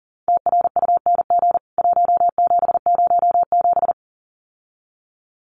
Morse code, heard as a keyed tone, about 40 wpm, 700 Hz. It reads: TRUNG 1707
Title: CWT Exchange 40wpm